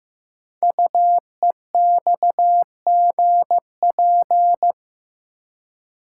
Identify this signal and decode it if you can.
Morse code, heard as a keyed tone, about 15 wpm, 700 Hz. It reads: UEXGP
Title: Code Groups - 5 Letters - 15wpm